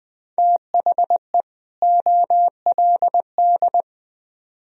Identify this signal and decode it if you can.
Morse code, heard as a keyed tone, about 25 wpm, 700 Hz. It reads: THE OLD
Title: Speed-Racing - Sets of 2 Words 20wpm